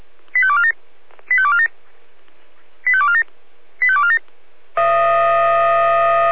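Five-tone selective call (ZVEI), heard as quick runs of five tones, tones of about 70 ms. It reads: 85247 85247 85247 85247
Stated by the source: ZVEI five-tone selective call 85246 received by SDR.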